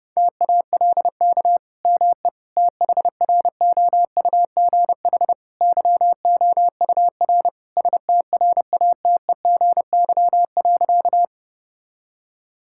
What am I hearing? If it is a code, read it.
Morse code, heard as a keyed tone, about 32 wpm, 700 Hz. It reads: TALK ME THROUGH YOUR STRATEGY.
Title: Sentences from Top 1000 Words - Review Part 5 - 30wpm